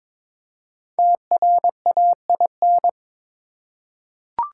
Morse code, heard as a keyed tone, about 22 wpm, 700 Hz. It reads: TRAIN
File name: 5 Letter Words - First 3 Letters Given - 22wpm